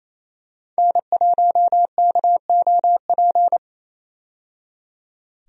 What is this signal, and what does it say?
Morse code, heard as a keyed tone, about 27 wpm, 700 Hz. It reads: N1KOP
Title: USA Call Signs - 5 Characters - 28wpm